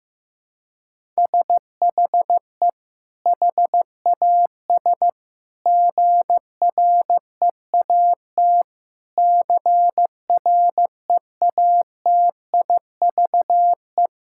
Morse code, heard as a keyed tone, about 15 wpm, 700 Hz. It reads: SHE HAS GREAT CREATIVE
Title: Sentences from Top 1000 Words - Review Part 6 - 15wpm